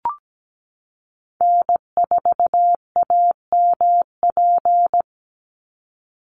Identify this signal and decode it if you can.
Morse code, heard as a keyed tone, about 17 wpm, 700 Hz. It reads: N4AMP
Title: USA Call Signs 17wpm